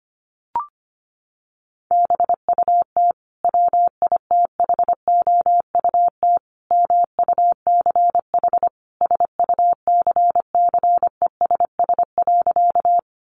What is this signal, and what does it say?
Morse code, heard as a keyed tone, about 25 wpm, 700 Hz. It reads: BUT WITHOUT MUCH SUCCESS.
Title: Sentences from Top 900 Words 25wpm